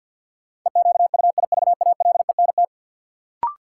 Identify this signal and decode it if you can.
Morse code, heard as a keyed tone, about 53 wpm, 700 Hz. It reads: EQUIVALENT
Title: Sentences from Top 1600 Words - New Word Drill Alt - 50wpm